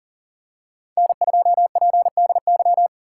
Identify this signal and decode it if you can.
Morse code, heard as a keyed tone, about 40 wpm, 700 Hz. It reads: N2PDY
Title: USA Call Signs - 5 Characters - 3x Character Spacing - 40wpm